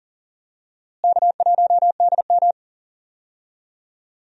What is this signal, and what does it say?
Morse code, heard as a keyed tone, about 39 wpm, 700 Hz. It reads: K1DM
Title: USA Call Signs - 4 Characters - Repeated 2x - 40wpm